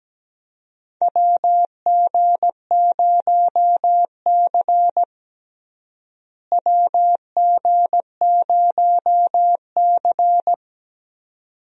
Morse code, heard as a keyed tone, about 17 wpm, 700 Hz. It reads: WG0C WG0C
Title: USA Call Signs - 4 Characters - Repeated 2x - 17wpm